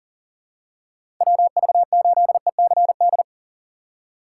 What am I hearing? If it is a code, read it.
Morse code, heard as a keyed tone, about 40 wpm, 700 Hz. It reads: WV8ECD
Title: USA Call Signs - 6 Characters - 6x Character Spacing - 40wpm